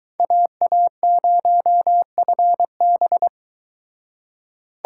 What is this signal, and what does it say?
Morse code, heard as a keyed tone, about 23 wpm, 700 Hz. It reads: AA0FB